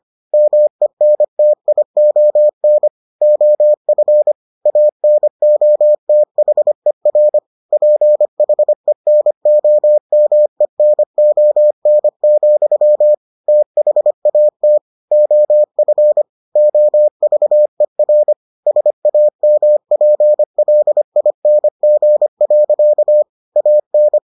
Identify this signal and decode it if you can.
Morse code, heard as a keyed tone, about 25 wpm, 600 Hz. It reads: MENTION OF ANOTHER PHENOMENON, THAT OF OVER SAMPLING. AN